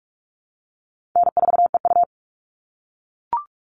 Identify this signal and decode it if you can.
Morse code, heard as a keyed tone, about 44 wpm, 700 Hz. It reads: N4EU
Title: All Cut Numbers - 4-Digit 45wpm